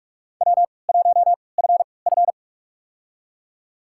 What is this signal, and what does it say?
Morse code, heard as a keyed tone, about 46 wpm, spaced as for 26 wpm, 700 Hz. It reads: W1FF
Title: USA Call Signs - 4 Characters - 3x Character Spacing - 45wpm